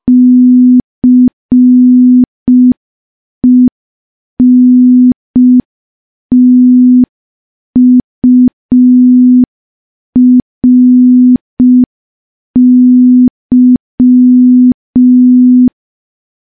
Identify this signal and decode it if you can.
Morse code, heard as a keyed tone, about 5 wpm, 250 Hz. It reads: CENTURY
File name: morse / morse-training-05WPM